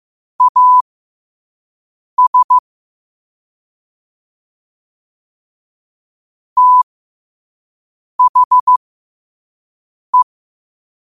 Morse code, heard as a keyed tone, about 15 wpm, spaced as for 5 wpm, 1000 Hz. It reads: AS THE
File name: qotdfiles / 5wpm